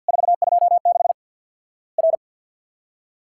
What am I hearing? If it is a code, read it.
Morse code, heard as a keyed tone, about 50 wpm, 700 Hz. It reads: 426 R
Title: Worked All Europe DX Contest - QTC Exchange with Repeats - 50wpm